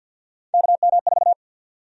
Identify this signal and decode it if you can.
Morse code, heard as a keyed tone, about 53 wpm, 700 Hz. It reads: KM4
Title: USA Call Sign Prefixes - 50wpm